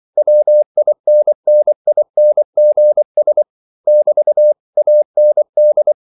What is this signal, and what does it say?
Morse code, heard as a keyed tone, about 24 wpm, 600 Hz. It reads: WINNINGS <BT> AND
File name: CW / Practice / If24